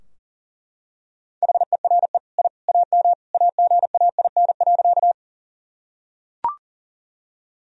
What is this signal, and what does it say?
Morse code, heard as a keyed tone, about 40 wpm, 700 Hz. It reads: HERE I AM AGAIN.
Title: Sentences from Top 1000 Words - Review Part 2 - 40wpm